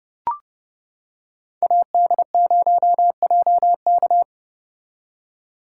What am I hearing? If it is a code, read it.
Morse code, heard as a keyed tone, about 29 wpm, 700 Hz. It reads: AD0JK